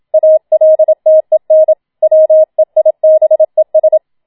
Morse code, heard as a keyed tone, about 27 wpm, 630 Hz. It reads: ALTEN WEIBES
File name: Schneewittchen Teil2 27wpm